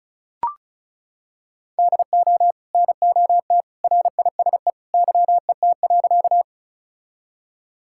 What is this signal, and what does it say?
Morse code, heard as a keyed tone, about 34 wpm, 700 Hz. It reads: DO NOT RISE YET.
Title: Sentences from Top 1000 Words - Review Part 6 - 35wpm